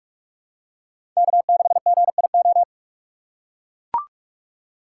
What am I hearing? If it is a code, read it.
Morse code, heard as a keyed tone, about 43 wpm, 700 Hz. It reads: K6GIO
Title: USA Call Signs - 3x Character Spacing - 45wpm